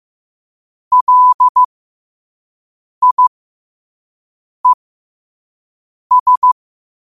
Morse code, heard as a keyed tone, about 15 wpm, spaced as for 5 wpm, 1000 Hz. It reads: LIES
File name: qotdfiles / 5wpm